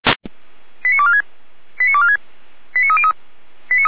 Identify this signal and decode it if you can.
Five-tone selective call (ZVEI), heard as quick runs of five tones, tones of about 70 ms. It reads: 89246 89246 89303